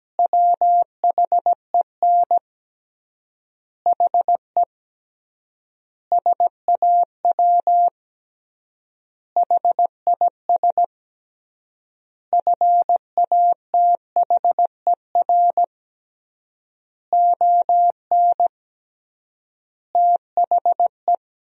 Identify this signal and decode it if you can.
Morse code, heard as a keyed tone, about 17 wpm, 700 Hz. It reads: WHEN HE SAW HIS FATHER ON THE
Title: 3x Word-Spacing - Sentences from Top 500 Words - 17wpm